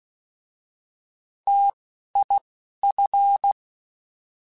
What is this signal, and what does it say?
Morse code, heard as a keyed tone, about 16 wpm, 800 Hz. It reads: T I F